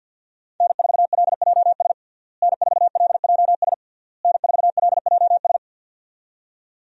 Morse code, heard as a keyed tone, about 53 wpm, 700 Hz. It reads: N4LJS N4LJS N4LJS
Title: USA Call Signs - 5 Characters - Repeated 3x - 50wpm